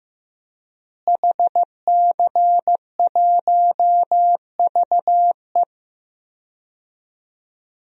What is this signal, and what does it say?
Morse code, heard as a keyed tone, about 15 wpm, 700 Hz. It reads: HC1VE
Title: DXCC-100 Call Signs - 5 Characters - Repeated 2x - 15wpm